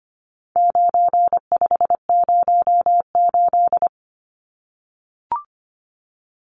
Morse code, heard as a keyed tone, about 25 wpm, 700 Hz. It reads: 9508
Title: Numbers - 4-Digit 25wpm